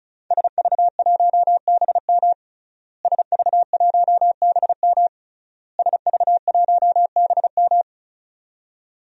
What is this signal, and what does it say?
Morse code, heard as a keyed tone, about 34 wpm, 700 Hz. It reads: SV1BM SV1BM SV1BM
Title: DXCC-100 Call Signs - 5 Characters - Repeated 3x - 35wpm